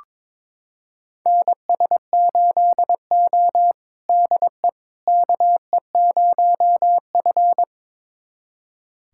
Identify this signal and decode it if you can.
Morse code, heard as a keyed tone, about 22 wpm, 700 Hz. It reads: NS8O DE KE0F
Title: USA Callsign DE Callsign - 4 Characters - 22wpm